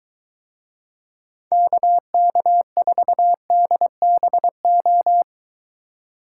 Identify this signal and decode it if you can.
Morse code, heard as a keyed tone, about 23 wpm, 700 Hz. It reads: KK4DBO